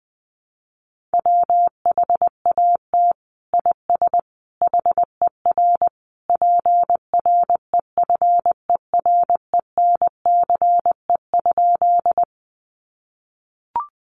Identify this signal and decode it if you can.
Morse code, heard as a keyed tone, about 20 wpm, 700 Hz. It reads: WHAT IS HER PREFERENCE?
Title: Sentences from Top 1500 Words 20wpm